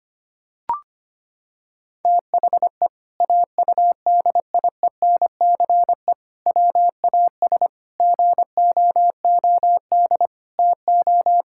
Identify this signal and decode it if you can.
Morse code, heard as a keyed tone, about 25 wpm, 700 Hz. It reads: THE AUDIENCE WAS GOOD TO
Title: Sentences from Top 1000 Words - Review Part 5 - 25wpm